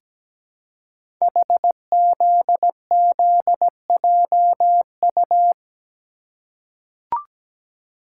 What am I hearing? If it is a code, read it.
Morse code, heard as a keyed tone, about 17 wpm, 700 Hz. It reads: HZZJU
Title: Code Groups - 5 Letters - 17wpm